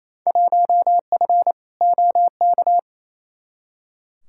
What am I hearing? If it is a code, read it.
Morse code, heard as a keyed tone, about 28 wpm, 700 Hz. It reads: 1F OK
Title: Field Day Exchanges Encore - 28wpm